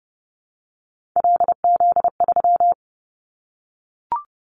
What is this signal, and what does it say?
Morse code, heard as a keyed tone, about 30 wpm, 700 Hz. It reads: LZ3
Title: Callsign Prefixes for Top 25 DXCC - 30wpm